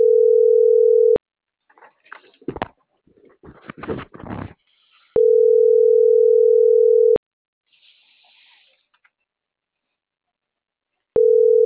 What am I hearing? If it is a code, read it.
Ringback tone (North America), 440 + 480 Hz.